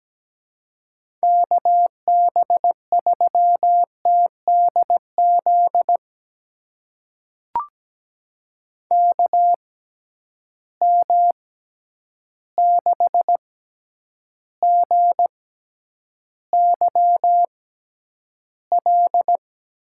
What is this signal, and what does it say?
Morse code, heard as a keyed tone, about 17 wpm, spaced as for 9 wpm, 700 Hz. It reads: KB3TDZ KM6GYL
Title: USA Call Signs - 6 Characters - 6x Character Spacing - 17wpm